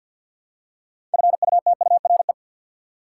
Morse code, heard as a keyed tone, about 49 wpm, 700 Hz. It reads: FUTURE
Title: Six Letter Words - 50wpm